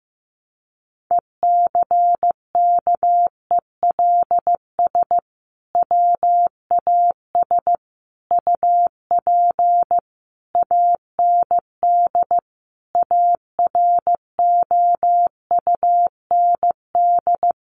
Morse code, heard as a keyed tone, about 15 wpm, 700 Hz. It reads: ECKELS WAS UP AND AROUND
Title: A Sound Of Thunder 15wpm